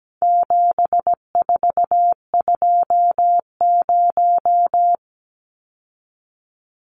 Morse code, heard as a keyed tone, about 17 wpm, 700 Hz. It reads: 7420
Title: Numbers - 4-Digit 17wpm